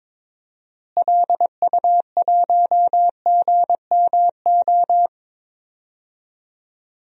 Morse code, heard as a keyed tone, about 22 wpm, 700 Hz. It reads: LU1GMO
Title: DXCC-100 Call Signs - 6 Characters - Repeated 2x - 22wpm